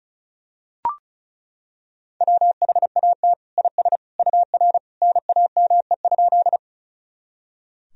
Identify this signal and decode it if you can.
Morse code, heard as a keyed tone, about 36 wpm, 700 Hz. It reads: WHAT IS UR NAME?
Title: Sentences from Top 200 Words and Abbreviations 35wpm